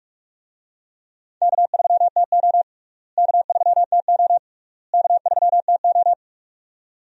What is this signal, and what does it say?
Morse code, heard as a keyed tone, about 43 wpm, 700 Hz. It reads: K3TO K3TO K3TO
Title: USA Call Signs - 4 Characters - Repeated 3x - 45wpm